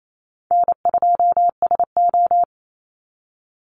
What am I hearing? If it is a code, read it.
Morse code, heard as a keyed tone, about 28 wpm, 700 Hz. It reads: N2SO
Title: Call Signs Repeated 2x - 28wpm